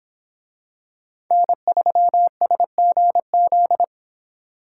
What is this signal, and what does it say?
Morse code, heard as a keyed tone, about 25 wpm, 700 Hz. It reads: N3SGZ